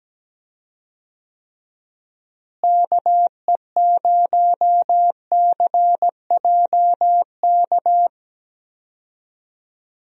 Morse code, heard as a keyed tone, about 17 wpm, 700 Hz. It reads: KE0CJK
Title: USA Call Signs - 6 Characters - Repeated 2x - 17wpm